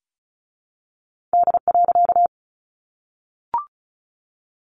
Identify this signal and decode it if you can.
Morse code, heard as a keyed tone, about 34 wpm, 700 Hz. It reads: D.
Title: A Sound Of Thunder 35wpm